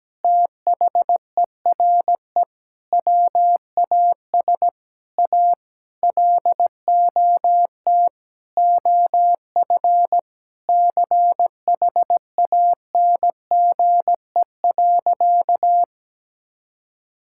Morse code, heard as a keyed tone, about 17 wpm, 700 Hz. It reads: THERE WAS A LOT OF CHANGE.
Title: Sentences from Top 1000 Words - Review Part 5 - 17wpm